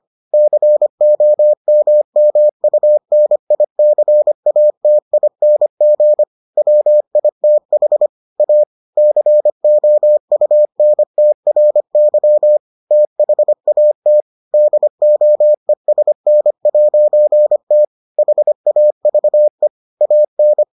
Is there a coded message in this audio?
Yes: Morse code from 0.3 to 20.6 s, heard as a keyed tone, about 25 wpm, 600 Hz. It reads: COMMUNICATING WITH A COUNTRY THAT DOESN'T HAVE AN